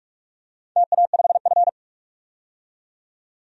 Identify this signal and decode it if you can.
Morse code, heard as a keyed tone, about 41 wpm, 700 Hz. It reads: TA5F
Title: DXCC-100 Call Signs - 4 Characters - 45wpm